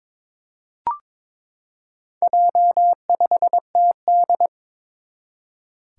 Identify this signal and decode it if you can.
Morse code, heard as a keyed tone, about 22 wpm, 700 Hz. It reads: J5TD
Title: Four Challenging Letter-Number Combinations 22wpm